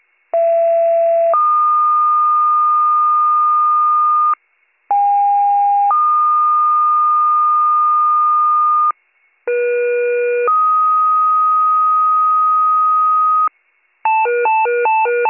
Two-tone page, heard as a one-second tone, then a three-second tone, about 670 then 1220 Hz, about 810 then 1220 Hz, about 480 then 1250 Hz. It is heard from 0.3 to 13.5 s.